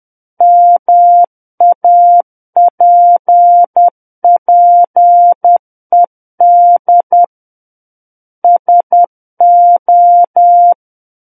Morse code, heard as a keyed tone, about 10 wpm, 700 Hz. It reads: MAPPED SO